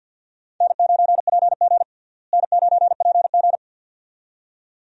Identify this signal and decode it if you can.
Morse code, heard as a keyed tone, about 48 wpm, 700 Hz. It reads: N9PG N9PG
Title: USA Call Signs - 4 Characters - Repeated 2x - 50wpm